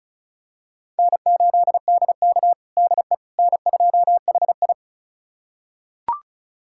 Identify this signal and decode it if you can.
Morse code, heard as a keyed tone, about 36 wpm, 700 Hz. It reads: N8DK DE N2HI
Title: USA Callsign DE Callsign - 4 Characters - 35wpm